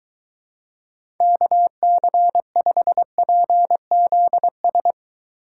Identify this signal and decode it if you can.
Morse code, heard as a keyed tone, about 23 wpm, 700 Hz. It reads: KC5PZS